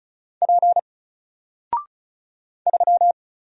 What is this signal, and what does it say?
Morse code, heard as a keyed tone, about 36 wpm, 700 Hz. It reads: P 3
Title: Single Letter-Number Alt 35wpm